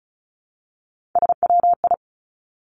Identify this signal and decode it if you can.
Morse code, heard as a keyed tone, about 36 wpm, 700 Hz. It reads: SWI
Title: Three Challenging Letter-Number Combinations 35wpm